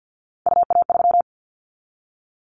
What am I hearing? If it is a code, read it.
Morse code, heard as a keyed tone, about 50 wpm, 700 Hz. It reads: UA3
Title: Callsign Prefixes for Top 50 DXCC - 50wpm